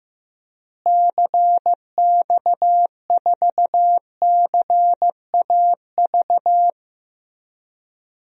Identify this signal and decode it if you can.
Morse code, heard as a keyed tone, about 15 wpm, 700 Hz. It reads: CX4CAV
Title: DXCC-100 Call Signs - 6 Characters - Repeated 3x - 15wpm